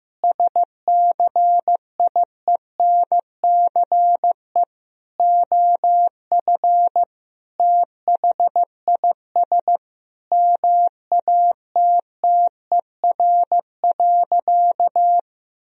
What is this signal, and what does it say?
Morse code, heard as a keyed tone, about 16 wpm, 700 Hz. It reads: SCIENCE OF THIS MATTER.
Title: Sentences from Top 1000 Words - Review Part 2 - 15wpm